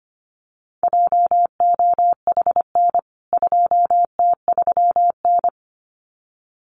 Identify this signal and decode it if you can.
Morse code, heard as a keyed tone, about 25 wpm, 700 Hz. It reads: JOHN 2T3N
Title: CWT Exchange 25wpm